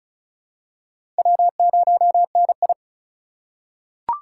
Morse code, heard as a keyed tone, about 34 wpm, 700 Hz. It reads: W0NI
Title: USA Call Signs - 4 Characters - 6x Character Spacing - 35wpm